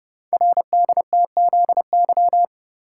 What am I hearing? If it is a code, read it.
Morse code, heard as a keyed tone, about 30 wpm, 700 Hz. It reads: RDTZY
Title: Code Groups - 5 Letters - 30wpm